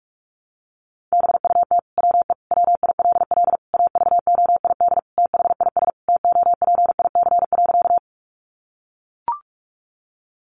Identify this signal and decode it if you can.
Morse code, heard as a keyed tone, about 44 wpm, 700 Hz. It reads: BUT WE WILL AVOID THIS TOPIC.
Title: Sentences from Top 1200 Words 45wpm